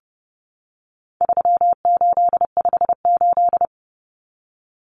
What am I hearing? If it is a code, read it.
Morse code, heard as a keyed tone, about 30 wpm, 700 Hz. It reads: 3858
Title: Common Cut Numbers - 4-Digit 30wpm